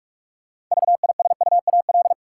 Morse code, heard as a keyed tone, about 41 wpm, 700 Hz. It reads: VISUAL
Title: Six Letter Words - 45wpm